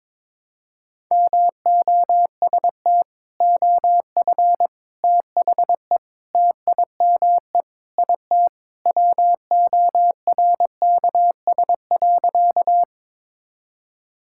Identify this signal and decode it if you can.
Morse code, heard as a keyed tone, about 22 wpm, 700 Hz. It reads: MOST OF THE TIME IT WORKS.
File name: practice / 22wpm / Sentences with 4-Word Phrases - 22wpm